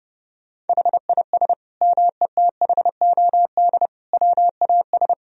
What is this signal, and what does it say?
Morse code, heard as a keyed tone, about 32 wpm, 700 Hz. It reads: HIS METHOD WAS
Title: Sentences from Top 1000 Words - Review Part 3 - 30wpm